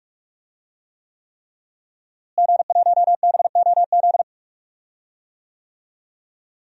Morse code, heard as a keyed tone, about 41 wpm, 700 Hz. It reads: G1BOZ
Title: DXCC-100 Call Signs - 5 Characters - Repeated 3x - 45wpm